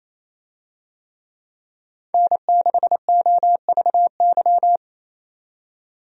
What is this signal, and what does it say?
Morse code, heard as a keyed tone, about 27 wpm, 700 Hz. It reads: N6OVY
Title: USA Call Signs - 5 Characters - 28wpm